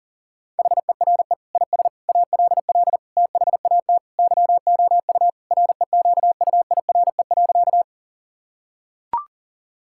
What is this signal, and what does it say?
Morse code, heard as a keyed tone, about 40 wpm, 700 Hz. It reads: HERE IS ALL THAT YOU REQUIRE.
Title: Sentences from Top 1000 Words - Review Part 6 - 40wpm